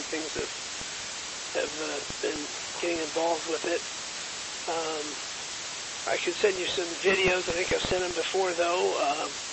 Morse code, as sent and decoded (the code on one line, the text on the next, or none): none